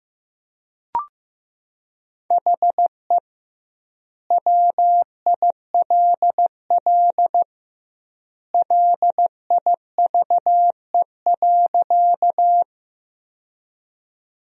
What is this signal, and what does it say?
Morse code, heard as a keyed tone, about 15 wpm, 700 Hz. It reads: HE WILL LIVE.
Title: 2x Word-Spacing - Sentences from Top 200 Words - 15wpm